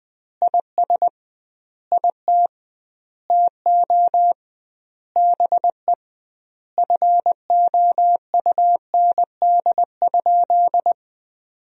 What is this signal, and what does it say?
Morse code, heard as a keyed tone, about 21 wpm, 700 Hz. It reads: IS IT TO BE FOUND?
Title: 2x Word-Spacing - Sentences from Top 300 Words - 20wpm